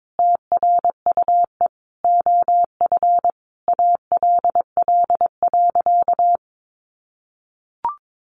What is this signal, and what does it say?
Morse code, heard as a keyed tone, about 22 wpm, 700 Hz. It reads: TRUE OF ALL.
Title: Sentences from Top 1400 Words 22wpm